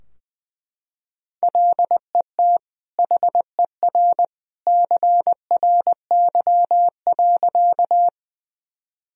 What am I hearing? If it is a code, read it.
Morse code, heard as a keyed tone, about 20 wpm, 700 Hz. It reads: LET HER CRY.
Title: Sentences from Top 1000 Words - Review Part 2 - 20wpm